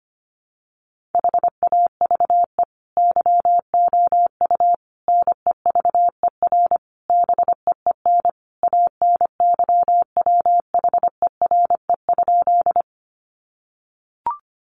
Morse code, heard as a keyed tone, about 25 wpm, 700 Hz. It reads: HAVE YOU NEVER BEEN ANYWHERE?
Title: Sentences from Top 1400 Words 25wpm